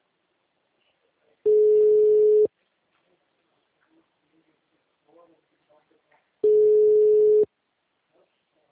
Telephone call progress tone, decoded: ringback tone

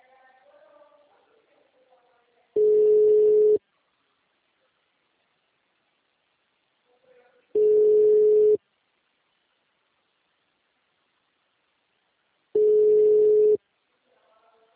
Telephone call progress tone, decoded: ringback tone

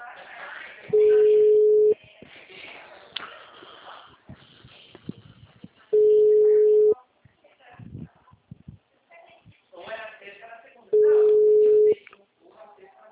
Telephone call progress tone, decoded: ringback tone

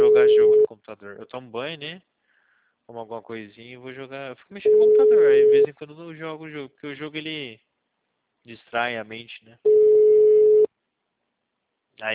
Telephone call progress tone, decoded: ringback tone

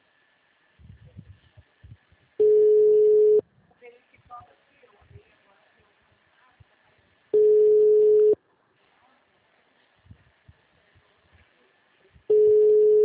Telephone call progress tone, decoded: ringback tone

